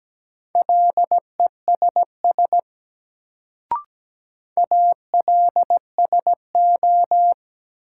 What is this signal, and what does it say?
Morse code, heard as a keyed tone, about 17 wpm, 700 Hz. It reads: LESS ALSO
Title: Sentences from Top 400 Words - New Word Drill - 17wpm